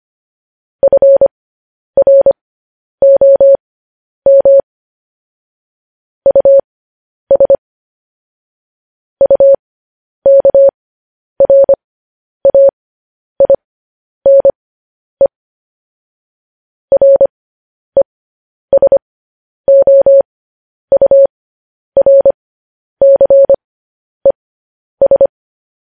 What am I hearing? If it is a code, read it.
Morse code, heard as a keyed tone, about 25 wpm, spaced as for 10 wpm, 550 Hz. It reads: FROM US UKRAINE RESOURCES
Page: news-headlines-in-morse-code-at-10-wpm